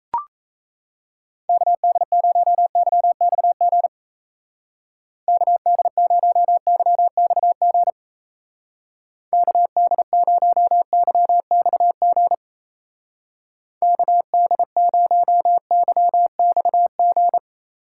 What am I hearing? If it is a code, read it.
Morse code, heard as a keyed tone, about 35 wpm, 700 Hz. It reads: KD0YXG KD0YXG KD0YXG KD0YXG
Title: Speed Racing - USA 2x3 Call Signs 28wpm